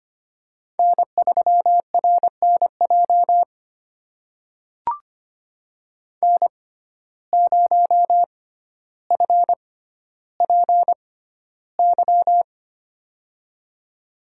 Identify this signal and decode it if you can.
Morse code, heard as a keyed tone, about 25 wpm, spaced as for 13 wpm, 700 Hz. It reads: N3RNJ N0FPY
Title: USA Call Signs - 5 Characters - 6x Character Spacing - 25wpm